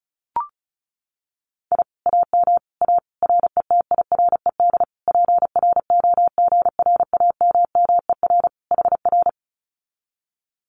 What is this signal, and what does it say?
Morse code, heard as a keyed tone, about 34 wpm, 700 Hz. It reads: I AM A RETIRED PROGRAMMER HR
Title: Common QSO Phrases Encore 35wpm